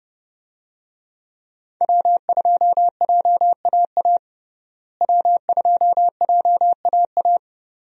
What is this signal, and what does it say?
Morse code, heard as a keyed tone, about 31 wpm, 700 Hz. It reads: W2JAA W2JAA
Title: USA Call Signs - 5 Characters - Repeated 2x - 30wpm